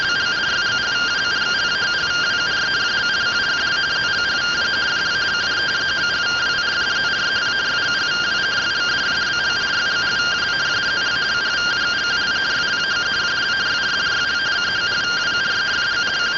Two-tone teleprinter signal, about 170 Hz shift.